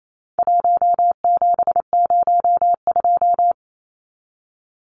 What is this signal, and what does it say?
Morse code, heard as a keyed tone, about 28 wpm, 700 Hz. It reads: 1702
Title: Numbers - 4-Digit 28wpm